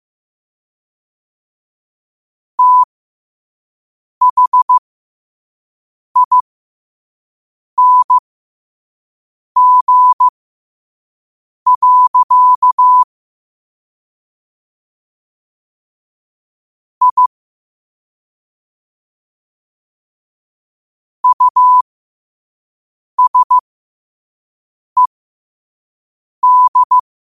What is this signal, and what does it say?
Morse code, heard as a keyed tone, about 15 wpm, spaced as for 5 wpm, 1000 Hz. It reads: THING. I USED
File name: qotdfiles / 5wpm